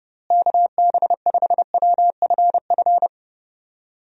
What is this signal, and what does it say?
Morse code, heard as a keyed tone, about 32 wpm, 700 Hz. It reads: KB5WFF
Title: USA Call Signs - 6 Characters - Repeated 3x - 30wpm